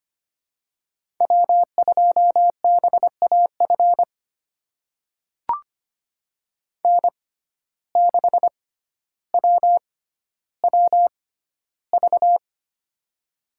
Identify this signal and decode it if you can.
Morse code, heard as a keyed tone, about 25 wpm, spaced as for 13 wpm, 700 Hz. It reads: W2BAF N6WWV
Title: USA Call Signs - 5 Characters - 6x Character Spacing - 25wpm